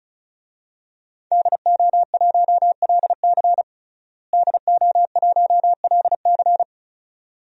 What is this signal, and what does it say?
Morse code, heard as a keyed tone, about 36 wpm, 700 Hz. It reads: DO1LC DO1LC
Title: DXCC-100 Call Signs - 5 Characters - Repeated 2x - 35wpm